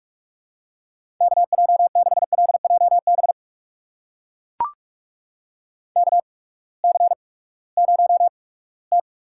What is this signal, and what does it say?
Morse code, heard as a keyed tone, about 44 wpm, spaced as for 22 wpm, 700 Hz. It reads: KJ6LJB KC0T
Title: USA Call Signs - 6 Characters - 8x Character Spacing - 45wpm